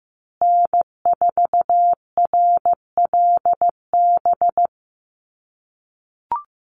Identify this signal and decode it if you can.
Morse code, heard as a keyed tone, about 15 wpm, 700 Hz. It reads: N4RLB
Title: USA Call Signs Repeated 2x 15wpm